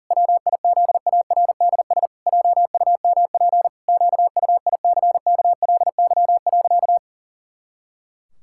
Morse code, heard as a keyed tone, about 40 wpm, 700 Hz. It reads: WIZARDS JUMP QUICKLY.
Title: Pangram Challenge v2 - 40wpm